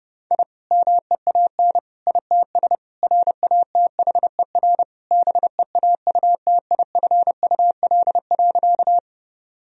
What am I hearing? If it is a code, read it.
Morse code, heard as a keyed tone, about 30 wpm, 700 Hz. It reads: I MEAN ITS RATHER BEAUTIFUL.
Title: Sentences from Top 1000 Words - Review Part 4 - 30wpm